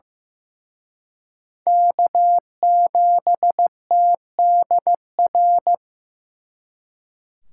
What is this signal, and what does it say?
Morse code, heard as a keyed tone, about 18 wpm, 700 Hz. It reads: K7TDR
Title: Speed Racing - USA 1x3 Call Signs 15wpm